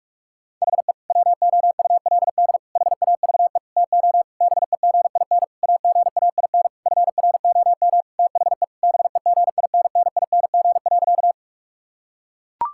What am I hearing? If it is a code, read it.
Morse code, heard as a keyed tone, about 41 wpm, 700 Hz. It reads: HE WOULD HAVE TO BEGIN AGAIN FROM THE BEGINNING.
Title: Sentences from Top 1000 Words - Review Part 5 - 45wpm